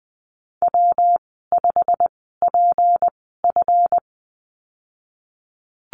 Morse code, heard as a keyed tone, about 20 wpm, spaced as for 15 wpm, 700 Hz. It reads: W5PF
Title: 161 Lesson 26g - Letter g - Call Signs 20wpm